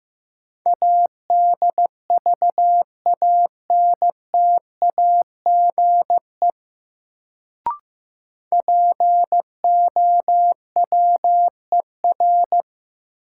Morse code, heard as a keyed tone, about 15 wpm, 700 Hz. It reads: ADVANTAGE POWER
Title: Sets of 4 Words - New Word Drill - 15wpm